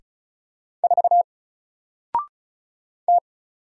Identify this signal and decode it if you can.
Morse code, heard as a keyed tone, about 36 wpm, 700 Hz. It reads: 4 T
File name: Single-Letter-Number-35wpm